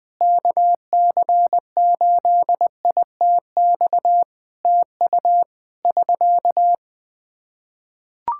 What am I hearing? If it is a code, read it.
Morse code, heard as a keyed tone, about 20 wpm, 700 Hz. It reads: KC8ITX TU <SK>
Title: Practice QSO - Exchange 4 - Ending - 20wpm